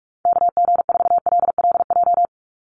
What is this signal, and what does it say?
Morse code, heard as a keyed tone, about 44 wpm, 700 Hz. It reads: KG4LLJ
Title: USA Call Signs 45wpm